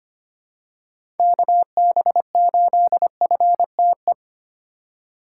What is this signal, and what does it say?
Morse code, heard as a keyed tone, about 25 wpm, 700 Hz. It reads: KB8FTE
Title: USA Call Signs - 3x Character Spacing - 25wpm